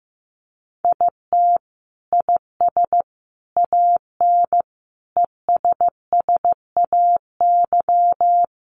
Morse code, heard as a keyed tone, about 15 wpm, 700 Hz. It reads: IT IS AN ESSAY
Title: Sentences from Top 1500 Words 15wpm